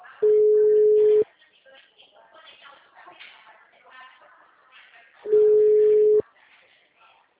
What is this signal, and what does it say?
Ringback tone (Europe, 425 Hz), with speech.